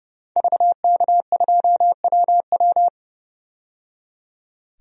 Morse code, heard as a keyed tone, about 30 wpm, 700 Hz. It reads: VK2WW
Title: DXCC-100 Call Signs - 5 Characters - 30wpm